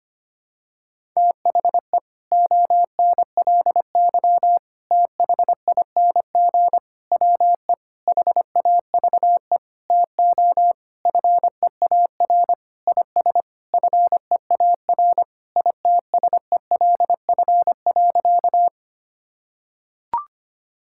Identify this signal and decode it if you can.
Morse code, heard as a keyed tone, about 25 wpm, 700 Hz. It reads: THE ONLY THING WE HAVE TO FEAR IS FEAR ITSELF.
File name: practice / 25wpm / Sentences with 9-Word Phrases - 25wpm